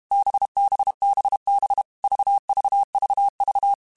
Morse code, heard as a keyed tone, about 32 wpm, 800 Hz. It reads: BBBB VVVV
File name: Groups@30 / Groups@30 mp3s / VB46@30